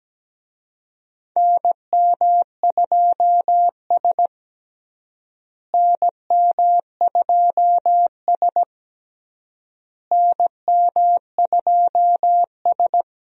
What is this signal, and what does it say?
Morse code, heard as a keyed tone, about 17 wpm, 700 Hz. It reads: NM2S NM2S NM2S
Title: USA Call Signs - 4 Characters - Repeated 3x - 17wpm